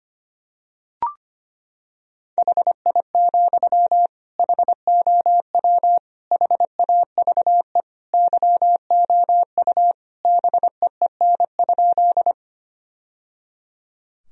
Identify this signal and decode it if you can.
Morse code, heard as a keyed tone, about 25 wpm, 700 Hz. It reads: HI, HOW HAVE YOU BEEN?
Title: Sentences from Top 1000 Words - Review Part 3 - 25wpm